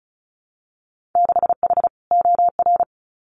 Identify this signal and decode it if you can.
Morse code, heard as a keyed tone, about 36 wpm, 700 Hz. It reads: 6H OR
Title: Winter Field Day Exchanges Encore 35wpm